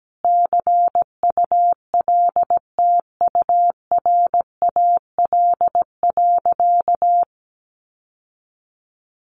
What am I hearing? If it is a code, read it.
Morse code, heard as a keyed tone, about 17 wpm, 700 Hz. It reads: CULTURAL.
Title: Sentences from Top 1500 Words 17wpm